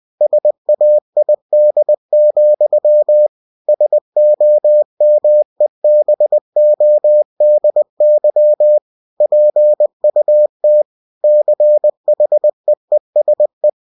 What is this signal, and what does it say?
Morse code, heard as a keyed tone, about 20 wpm, 600 Hz. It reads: SAID, SOMEBODY PUT CHEESE
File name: cw-books / winnie-the-pooh / winnie-20wpm-page-0037